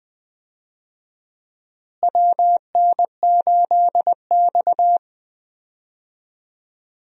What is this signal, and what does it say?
Morse code, heard as a keyed tone, about 20 wpm, 690 Hz. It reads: WN8X
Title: USA Call Signs - 4 Characters - Repeated 3x - 20wpm